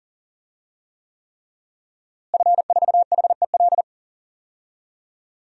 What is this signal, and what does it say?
Morse code, heard as a keyed tone, about 40 wpm, 700 Hz. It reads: F4HEL
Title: DXCC-100 Call Signs - 5 Characters - Repeated 3x - 40wpm